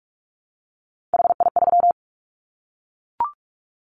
Morse code, heard as a keyed tone, about 44 wpm, 700 Hz. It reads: HI3
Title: Callsign Prefixes for Top 100 DXCC - 45wpm